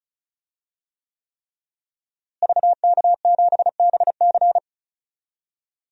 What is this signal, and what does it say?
Morse code, heard as a keyed tone, about 36 wpm, 700 Hz. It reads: VK7BC
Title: DXCC-100 Call Signs - 5 Characters - Repeated 3x - 35wpm